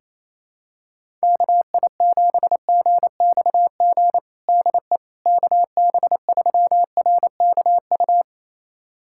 Morse code, heard as a keyed tone, about 28 wpm, 700 Hz. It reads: KI7GXG DE KB3RKU
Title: USA Callsign DE Callsign 28wpm